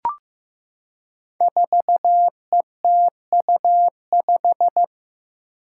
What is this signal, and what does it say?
Morse code, heard as a keyed tone, about 15 wpm, 700 Hz. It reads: 4ETU5
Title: Five Challenging Letter-Number Combinations 15wpm